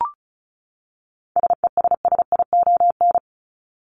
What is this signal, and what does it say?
Morse code, heard as a keyed tone, about 34 wpm, 700 Hz. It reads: SESSION
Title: Top 1501-1750 Words 35wpm